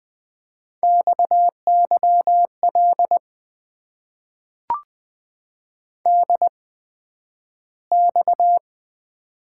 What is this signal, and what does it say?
Morse code, heard as a keyed tone, about 20 wpm, spaced as for 11 wpm, 700 Hz. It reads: XYL DX
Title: Top 90 QSO Elements - 8x Character Spacing - 20wpm